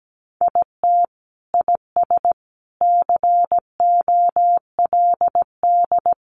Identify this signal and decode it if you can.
Morse code, heard as a keyed tone, about 17 wpm, 700 Hz. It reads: IT IS COLD
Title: Sentences from Top 1400 Words 17wpm